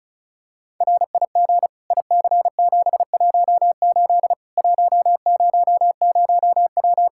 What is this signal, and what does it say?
Morse code, heard as a keyed tone, about 36 wpm, 700 Hz. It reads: RIG IC718 100W
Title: K1USN SST QSO Practice 35wpm